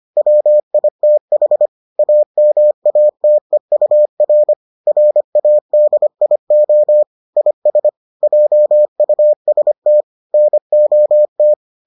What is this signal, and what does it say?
Morse code, heard as a keyed tone, about 25 wpm, 600 Hz. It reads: WITH AMATEUR RADIO IS JUST NOT